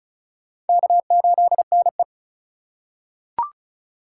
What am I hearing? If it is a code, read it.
Morse code, heard as a keyed tone, about 36 wpm, 700 Hz. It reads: K8NE
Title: USA Call Signs - 4x Character Spacing - 35wpm